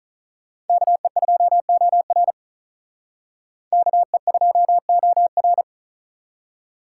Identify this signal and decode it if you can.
Morse code, heard as a keyed tone, about 45 wpm, 700 Hz. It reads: KE2OR KE2OR